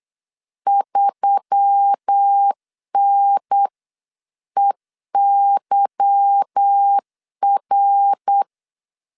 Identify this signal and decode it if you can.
Morse code, heard as a keyed tone, about 8 wpm, 800 Hz. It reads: 3NEYR
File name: LEZIONI / 8 YJ45 / 8 YJ45 8wpm 3214